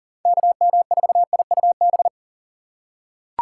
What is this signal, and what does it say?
Morse code, heard as a keyed tone, about 40 wpm, 700 Hz. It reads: KM4IUB
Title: USA Call Signs - 6x Character Spacing - 40wpm